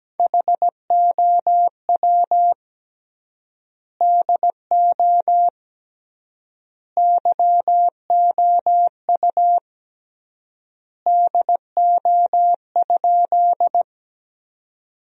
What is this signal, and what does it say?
Morse code, heard as a keyed tone, about 17 wpm, 700 Hz. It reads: HOW DO YOU DO?
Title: 3x Word-Spacing - Sentences from Top 100 Words - 17wpm